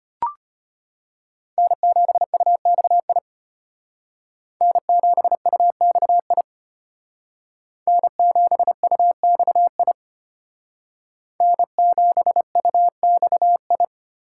Morse code, heard as a keyed tone, about 31 wpm, 700 Hz. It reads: N7UXI N7UXI N7UXI N7UXI
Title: Speed Racing - USA Call Signs 25wpm